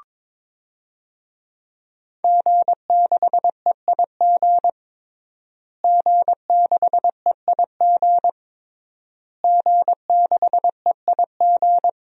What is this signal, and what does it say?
Morse code, heard as a keyed tone, about 22 wpm, 700 Hz. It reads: G6EIG G6EIG G6EIG
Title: DXCC-100 Call Signs - 5 Characters - Repeated 3x - 22wpm